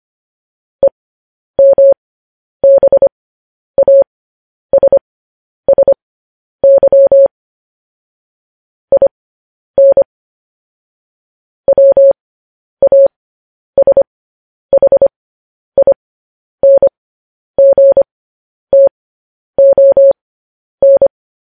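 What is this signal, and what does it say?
Morse code, heard as a keyed tone, about 25 wpm, spaced as for 10 wpm, 550 Hz. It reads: EMBASSY IN WASHINGTON
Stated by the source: Morse code news headlines practice, at 10 WPM.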